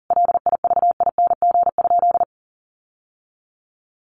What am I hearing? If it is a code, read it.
Morse code, heard as a keyed tone, about 40 wpm, 700 Hz. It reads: LIVING?
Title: Sentences from Top 1100 Words 40wpm